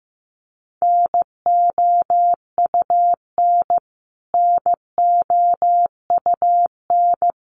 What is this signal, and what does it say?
Morse code, heard as a keyed tone, about 15 wpm, 700 Hz. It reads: NOUN NOUN
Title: Top 500 Words Repeated 15wpm